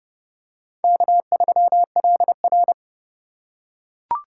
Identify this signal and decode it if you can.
Morse code, heard as a keyed tone, about 31 wpm, 700 Hz. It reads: K3LR
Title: K1USN SST QSO Practice - 30wpm